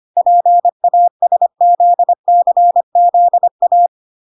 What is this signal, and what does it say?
Morse code, heard as a keyed tone, about 25 wpm, 700 Hz. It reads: PASZCZA
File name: 25wpm